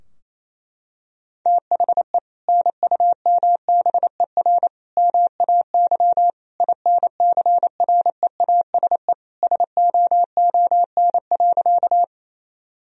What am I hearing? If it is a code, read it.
Morse code, heard as a keyed tone, about 27 wpm, 700 Hz. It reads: THE NUMBER MAY INCREASE SOON.
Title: Sentences from Top 1000 Words - Review Part 4 - 28wpm